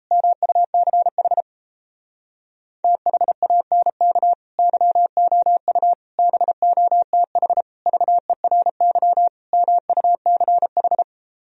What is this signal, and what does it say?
Morse code, heard as a keyed tone, about 38 wpm, 700 Hz. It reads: MUCH THANK YOU BOTH VERY MUCH